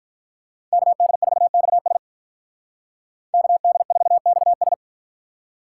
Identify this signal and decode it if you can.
Morse code, heard as a keyed tone, about 51 wpm, 700 Hz. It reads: KD4XS KD4XS